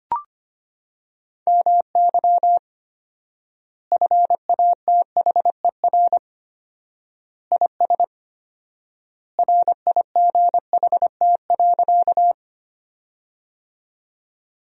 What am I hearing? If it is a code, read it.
Morse code, heard as a keyed tone, about 25 wpm, 700 Hz. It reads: MY FATHER IS RIGHT.
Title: 4x Word-Spacing - Sentences from Top 300 Words - 25wpm